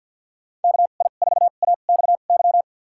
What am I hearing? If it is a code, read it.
Morse code, heard as a keyed tone, about 51 wpm, spaced as for 36 wpm, 700 Hz. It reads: KI4AXY
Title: USA Call Signs - 2x Character Spacing - 50wpm